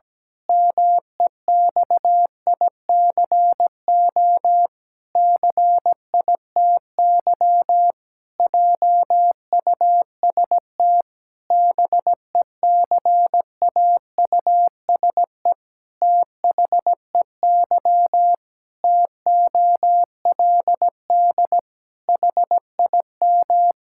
Morse code, heard as a keyed tone, about 17 wpm, 700 Hz. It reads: MEXICO CITY JUST BECAUSE THEY TOLD HIM
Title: Pangram Challenge v2 - 17wpm